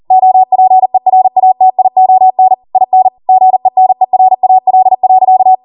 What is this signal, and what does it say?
Morse code, heard as a keyed tone, about 40 wpm, 750 Hz. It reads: OPERATION IN GENERAL.